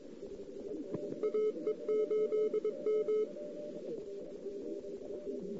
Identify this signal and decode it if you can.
Morse code, heard as a keyed tone, about 21 wpm, 2150 Hz, with noise. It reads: AE8M
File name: CW / 2008-ss-cw-k5tr-op-n5rz -14:00